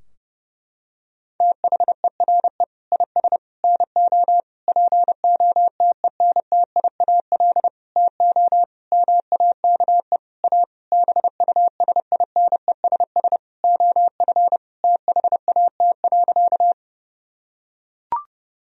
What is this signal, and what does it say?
Morse code, heard as a keyed tone, about 30 wpm, 700 Hz. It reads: THERE IS NO POTENTIAL TO MAKE A BUSINESS OF THAT.
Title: Sentences from Top 1000 Words - Review Part 4 - 30wpm